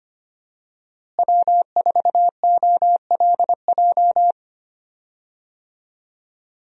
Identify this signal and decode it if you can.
Morse code, heard as a keyed tone, about 25 wpm, 700 Hz. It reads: W4OLJ
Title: USA Call Signs - 5 Characters - Repeated 2x - 25wpm